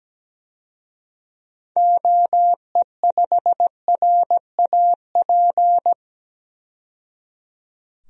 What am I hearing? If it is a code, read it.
Morse code, heard as a keyed tone, about 17 wpm, 700 Hz. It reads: OE5RAP
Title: DXCC-100 Call Signs - 6 Characters - 17wpm